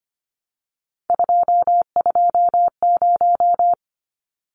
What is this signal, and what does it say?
Morse code, heard as a keyed tone, about 25 wpm, 700 Hz. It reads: 220
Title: Numbers - 3-Digit 25wpm